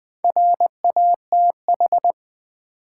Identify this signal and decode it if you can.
Morse code, heard as a keyed tone, about 20 wpm, 690 Hz. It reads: RATH
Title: Build A Word - 2 Letters At A Time - 6-Letter Words - 20wpm